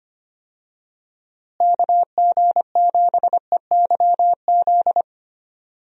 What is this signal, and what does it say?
Morse code, heard as a keyed tone, about 25 wpm, 700 Hz. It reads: KG7EYZ
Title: USA Call Signs - 6 Characters - Repeated 3x - 25wpm